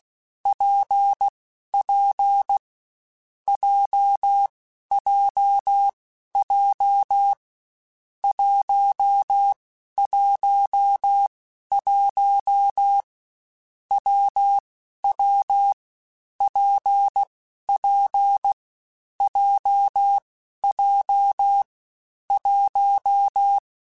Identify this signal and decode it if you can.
Morse code, heard as a keyed tone, about 16 wpm, 800 Hz. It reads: P P J J J 1 1 1 W W P P J J 1
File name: Difficult Combos / Difficult Combos-mp3s / WPJ WPJ1 @15